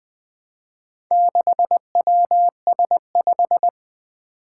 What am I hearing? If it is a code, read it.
Morse code, heard as a keyed tone, about 20 wpm, 700 Hz. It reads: 6WS5
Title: Four Challenging Letter-Number Combinations 20wpm